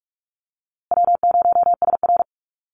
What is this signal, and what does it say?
Morse code, heard as a keyed tone, about 44 wpm, 700 Hz. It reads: W0SR
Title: USA Call Signs Repeated 3x 45wpm